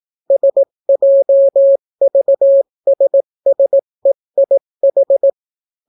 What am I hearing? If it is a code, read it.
Morse code, heard as a keyed tone, about 18 wpm, 550 Hz. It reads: SJVSSEIH